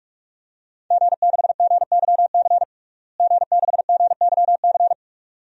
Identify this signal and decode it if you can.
Morse code, heard as a keyed tone, about 46 wpm, 700 Hz. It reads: G6GYC G6GYC
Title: DXCC-100 Call Signs - 5 Characters - Repeated 2x - 45wpm